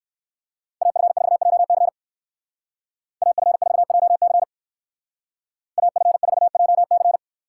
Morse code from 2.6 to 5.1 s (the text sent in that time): AF4JC